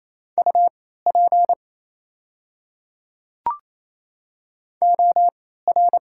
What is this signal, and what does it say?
Morse code, heard as a keyed tone, about 27 wpm, spaced as for 16 wpm, 690 Hz. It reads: UP OR
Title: 3x Character Spacing - Top 300 Words - 28wpm